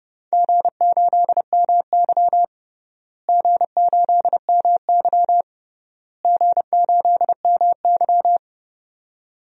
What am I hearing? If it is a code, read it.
Morse code, heard as a keyed tone, about 30 wpm, 700 Hz. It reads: G8MY G8MY G8MY
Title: DXCC-100 Call Signs - 4 Characters - Repeated 3x - 30wpm